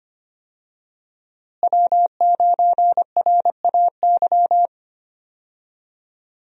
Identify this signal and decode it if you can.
Morse code, heard as a keyed tone, about 25 wpm, 700 Hz. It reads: W9RAY
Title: USA Call Signs - 5 Characters - Repeated 2x - 25wpm